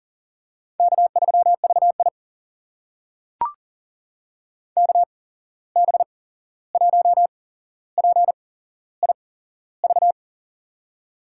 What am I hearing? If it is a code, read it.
Morse code, heard as a keyed tone, about 40 wpm, spaced as for 15 wpm, 700 Hz. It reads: K3VI KB1PIV
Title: USA Call Signs - 8x Character Spacing - 40wpm